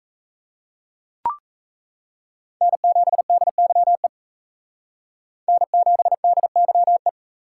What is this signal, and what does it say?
Morse code, heard as a keyed tone, about 36 wpm, 700 Hz. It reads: N7DYE N7DYE